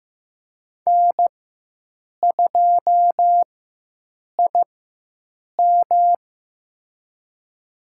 Morse code, heard as a keyed tone, about 15 wpm, spaced as for 7 wpm, 700 Hz. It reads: N2IM
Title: USA Call Signs - 4x Character Spacing - 15wpm